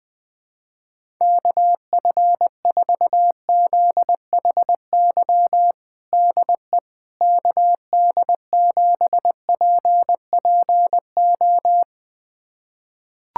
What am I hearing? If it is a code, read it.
Morse code, heard as a keyed tone, about 20 wpm, 700 Hz. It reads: KF4ZHY DE KD7PPO
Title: USA Callsign DE Callsign 20wpm